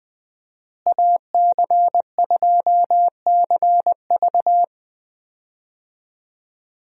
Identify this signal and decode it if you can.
Morse code, heard as a keyed tone, about 20 wpm, 700 Hz. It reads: AC2CV
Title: USA Call Signs - 5 Characters - Repeated 3x - 20wpm